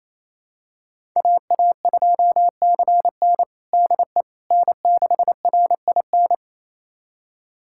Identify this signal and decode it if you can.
Morse code, heard as a keyed tone, about 28 wpm, 700 Hz. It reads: AA2CN DE N6RIN
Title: USA Callsign DE Callsign - 5 Characters - 28wpm